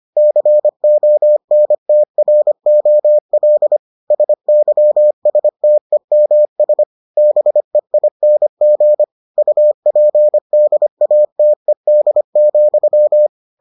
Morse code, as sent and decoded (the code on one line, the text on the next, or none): -.-. --- -. - .-. --- .-.. / ... -.-- ... - . -- ... / -... .. -. --. / ..- .--. -.. .- - . -.. --..--
CONTROL SYSTEMS BEING UPDATED,